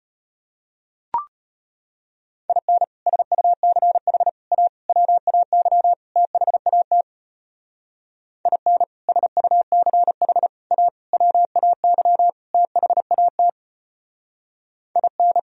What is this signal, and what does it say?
Morse code, heard as a keyed tone, about 30 wpm, 700 Hz. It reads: IN SUCH A WAY THAT IN SUCH A WAY THAT IN
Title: Speed-Racing - Sets of 5 Words 25wpm